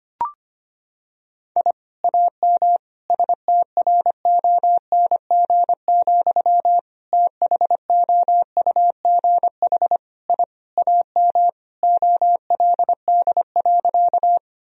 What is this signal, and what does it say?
Morse code, heard as a keyed tone, about 25 wpm, 700 Hz. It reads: I AM STRONG, THOUGH I AM OLD.
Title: Sentences from Top 1000 Words - Review Part 3 - 25wpm